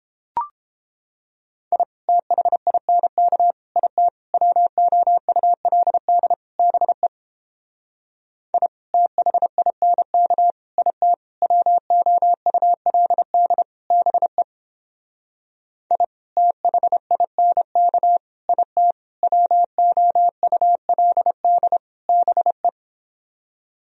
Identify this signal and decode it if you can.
Morse code, heard as a keyed tone, about 29 wpm, 700 Hz. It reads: I THINK IT WOULD BE I THINK IT WOULD BE I THINK IT WOULD BE